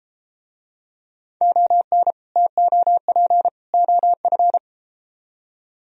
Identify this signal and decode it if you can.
Morse code, heard as a keyed tone, about 31 wpm, 700 Hz. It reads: ON TOP OF